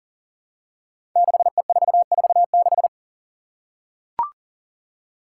Morse code, heard as a keyed tone, about 40 wpm, 700 Hz. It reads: 6E446
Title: Five Challenging Letter-Number Combinations 40wpm